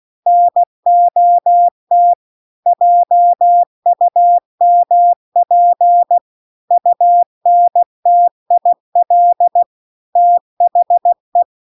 Morse code, heard as a keyed tone, about 16 wpm, 700 Hz. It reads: NOT JUMP UNTIL THE
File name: CW-The-Big-Twoheaded-River-2-16WPM